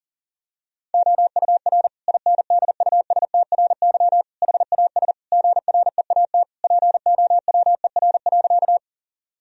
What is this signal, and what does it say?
Morse code, heard as a keyed tone, about 37 wpm, 700 Hz. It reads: OUR INDUSTRY HAS GREAT POWER.
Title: Sentences from Top 1000 Words - Review Part 4 - 40wpm